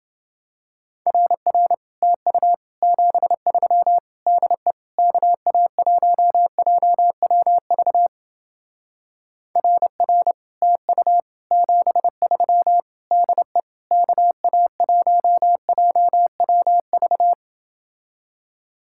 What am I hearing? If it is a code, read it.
Morse code, heard as a keyed tone, about 28 wpm, 700 Hz. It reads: RR TU 73 DE KA1JWV RR TU 73 DE KA1JWV